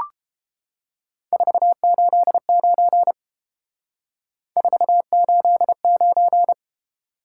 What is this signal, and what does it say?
Morse code, heard as a keyed tone, about 28 wpm, 700 Hz. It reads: 489 489